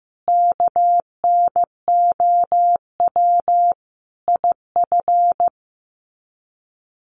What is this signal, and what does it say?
Morse code, heard as a keyed tone, about 15 wpm, 700 Hz. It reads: KNOW IF
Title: Sets of 5 Words Encore2 15wpm